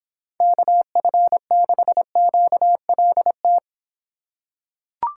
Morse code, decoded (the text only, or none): KF6QLT